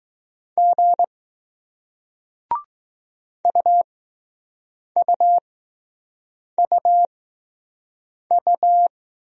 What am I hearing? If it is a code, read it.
Morse code, heard as a keyed tone, about 20 wpm, spaced as for 6 wpm, 700 Hz. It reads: G UUUU